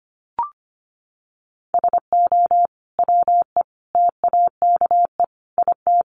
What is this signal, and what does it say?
Morse code, heard as a keyed tone, about 25 wpm, 700 Hz. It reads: SO WE TAKE IT
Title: Sentences from Top 700 Words 25wpm